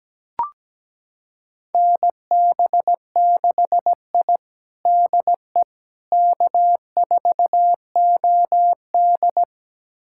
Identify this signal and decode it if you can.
Morse code, heard as a keyed tone, about 17 wpm, 700 Hz. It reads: NB6I DE K4OD
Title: USA Callsign DE Callsign - 4 Characters - 17wpm